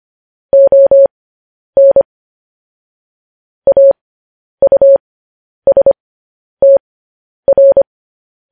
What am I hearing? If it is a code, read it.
Morse code, heard as a keyed tone, about 25 wpm, spaced as for 10 wpm, 550 Hz. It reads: ON AUSTR
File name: podcast / News headlines-1x-25x10